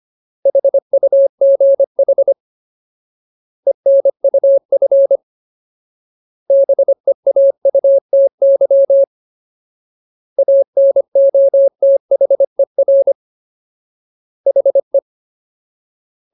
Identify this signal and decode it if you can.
Morse code, heard as a keyed tone, about 25 wpm, 550 Hz. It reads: HUGH ENUF BEAUTY ANOTHER HE
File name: Ultimate Word List / 25WPM Ultimate